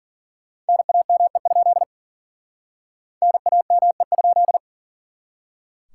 Morse code, heard as a keyed tone, about 50 wpm, 700 Hz. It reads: NAME? NAME?
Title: Speed-Racing Common QSO Phrases - 40wpm